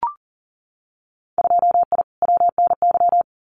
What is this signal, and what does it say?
Morse code, heard as a keyed tone, about 42 wpm, 700 Hz. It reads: 2I WNY
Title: Winter Field Day Exchanges Encore 40wpm